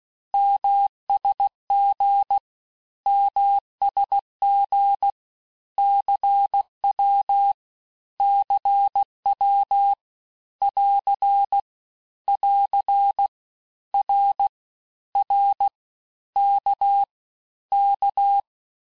Morse code, heard as a keyed tone, about 16 wpm, 800 Hz. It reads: MSG MSG CW CW <AR> <AR> R R K K
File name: cours-du-lundi files / Cours-1586-texte-abreviations-vitesse-15mpm-puis-20mpm